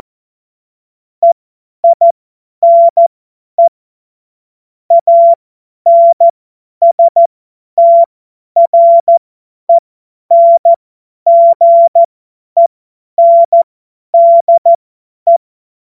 Morse code, heard as a keyed tone, about 14 wpm, spaced as for 10 wpm, 680 Hz. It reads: EINE ANSTRENGENDE
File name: Morsekurs / Witze / Witz 29. Juni 2021, 50 F70